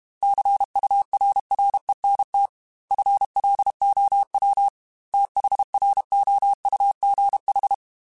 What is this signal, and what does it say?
Morse code, heard as a keyed tone, about 32 wpm, 800 Hz. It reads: CURRENT FLOW THROUGH
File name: Technical Subjects / Technical Subjects-mp3s / Reactance @30